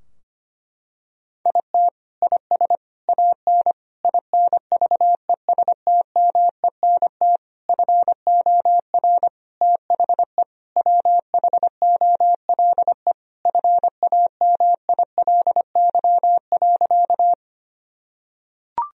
Morse code, heard as a keyed tone, about 25 wpm, 700 Hz. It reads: IT IS AN INVESTMENT FOR THE WHOLE FAMILY.
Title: Sentences from Top 1000 Words - Review Part 4 - 25wpm